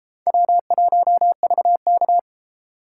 Morse code, heard as a keyed tone, about 32 wpm, 700 Hz. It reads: W1VK